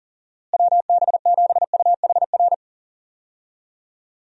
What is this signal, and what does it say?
Morse code, heard as a keyed tone, about 37 wpm, 700 Hz. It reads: WB7UHR
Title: USA Call Signs - 6 Characters - Repeated 3x - 40wpm